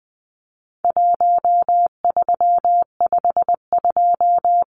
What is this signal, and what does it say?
Morse code, heard as a keyed tone, about 20 wpm, 700 Hz. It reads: 1352
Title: Numbers - Contest Style - 4-Digit 20wpm